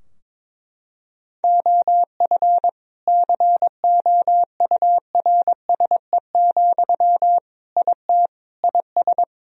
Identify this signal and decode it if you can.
Morse code, heard as a keyed tone, about 21 wpm, 700 Hz. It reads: OF COURSE, IT IS
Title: Sentences from Top 1000 Words - Review Part 2 - 22wpm